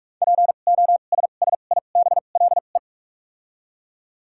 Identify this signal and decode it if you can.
Morse code, heard as a keyed tone, about 48 wpm, 700 Hz. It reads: P O S S I B L E
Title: 2x Character Spacing - Top 500 Words - 45wpm